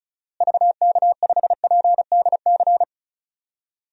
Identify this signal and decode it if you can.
Morse code, heard as a keyed tone, about 34 wpm, 700 Hz. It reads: VK5PDC
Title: DXCC-100 Call Signs - 6 Characters - Repeated 2x - 35wpm